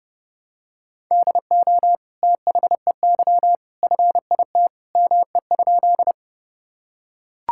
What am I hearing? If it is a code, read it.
Morse code, heard as a keyed tone, about 30 wpm, 700 Hz. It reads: DO THEY FIT ME?
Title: Sentences from Top 1000 Words - Review Part 5 - 30wpm